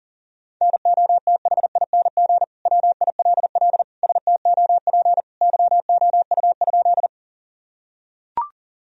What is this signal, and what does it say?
Morse code, heard as a keyed tone, about 40 wpm, 700 Hz. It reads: NOTHING WILL STOP YOU?
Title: Sentences from Top 1000 Words - Review Part 1 - 40wpm